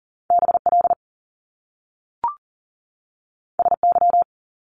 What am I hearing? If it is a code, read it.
Morse code, heard as a keyed tone, about 38 wpm, 700 Hz. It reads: BL SY
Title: Top Two Letter Combinations in Words 40wpm